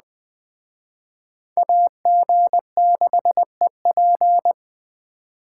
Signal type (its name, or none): Morse code